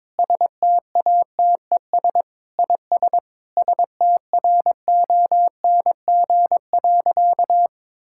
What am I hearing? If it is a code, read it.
Morse code, heard as a keyed tone, about 22 wpm, 700 Hz. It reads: STATES IS STRONG.
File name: practice / 22wpm / Sentences with 8-Word Phrases - 22wpm